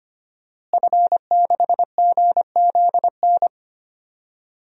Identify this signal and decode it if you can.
Morse code, heard as a keyed tone, about 25 wpm, 700 Hz. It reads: F6GZN
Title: DXCC-100 Call Signs - 5 Characters - Repeated 3x - 25wpm